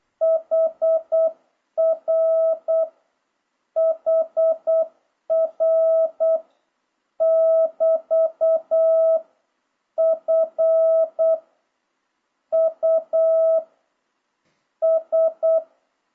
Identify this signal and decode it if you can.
Morse code, heard as a keyed tone, about 8 wpm, 660 Hz. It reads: HR HR <BT> F U S
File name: jh8cbh / 2026cw